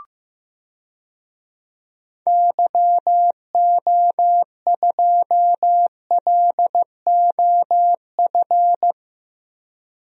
Morse code, heard as a keyed tone, about 15 wpm, 700 Hz. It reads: YO2LOF
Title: DXCC-100 Call Signs - 6 Characters - 15wpm